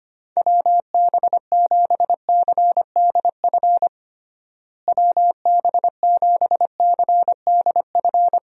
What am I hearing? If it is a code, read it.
Morse code, heard as a keyed tone, about 25 wpm, 700 Hz. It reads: WB7CDF WB7CDF
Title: USA Call Signs - 6 Characters - Repeated 3x - 25wpm